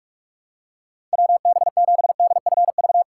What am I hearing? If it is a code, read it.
Morse code, heard as a keyed tone, about 45 wpm, 700 Hz. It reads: WB7DFV